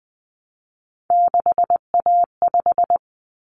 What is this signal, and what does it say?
Morse code, heard as a keyed tone, about 20 wpm, 700 Hz. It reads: 6A5
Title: Common Cut Numbers - 3-Digit 20wpm